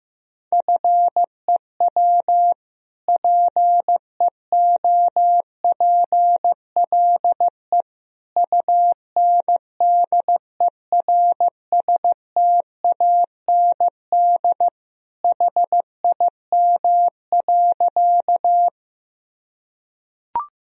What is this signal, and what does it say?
Morse code, heard as a keyed tone, about 15 wpm, 700 Hz. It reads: FEW PEOPLE UNDERSTAND HIM.
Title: Sentences from Top 1000 Words - Review Part 1 - 15wpm